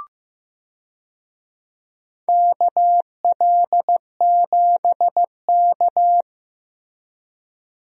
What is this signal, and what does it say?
Morse code, heard as a keyed tone, about 15 wpm, 700 Hz. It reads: KL7K
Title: DXCC-100 Call Signs - 4 Characters - Repeated 3x - 15wpm